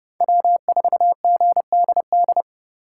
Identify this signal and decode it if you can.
Morse code, heard as a keyed tone, about 30 wpm, 700 Hz. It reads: W4GDD
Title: USA Call Signs - 5 Characters - 3x Character Spacing - 30wpm